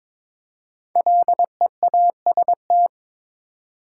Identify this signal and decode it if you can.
Morse code, heard as a keyed tone, about 22 wpm, 700 Hz. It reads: LEAST
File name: 5 Letter Words - First 3 Letters Given - 22wpm